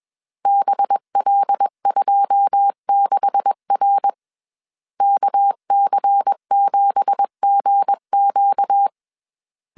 Morse code, heard as a keyed tone, about 21 wpm, 800 Hz. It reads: BL26R KC7GQ
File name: CW / DET / 1 9 1500C S C 20wpm 2310